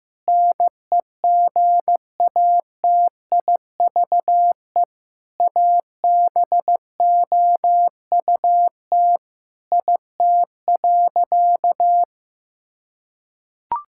Morse code, heard as a keyed tone, about 15 wpm, 700 Hz. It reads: NEGATIVE ABOUT IT.
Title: Sentences from Top 1000 Words - Review Part 6 - 15wpm